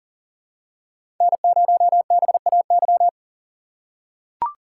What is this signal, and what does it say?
Morse code, heard as a keyed tone, about 40 wpm, 700 Hz. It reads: N0BAY
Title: USA Call Signs - 5 Characters - 3x Character Spacing - 40wpm